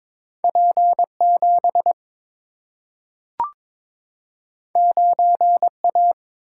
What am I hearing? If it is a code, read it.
Morse code, heard as a keyed tone, about 22 wpm, 700 Hz. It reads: P7 9A
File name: practice / 22wpm / Two Letter-Number Combinations - 22wpm